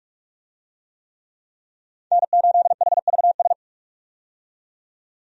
Morse code, heard as a keyed tone, about 42 wpm, 700 Hz. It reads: N8HVS
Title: USA Call Signs - 5 Characters - 45wpm